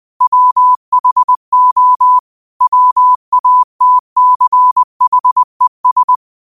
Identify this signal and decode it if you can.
Morse code, heard as a keyed tone, about 20 wpm, 1000 Hz. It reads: WHO WATCHES